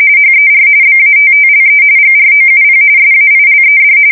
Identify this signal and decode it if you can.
RTTY, about 45.45 baud. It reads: HER BATTLE GROUP SAILED
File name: fsk170 / battleshipnj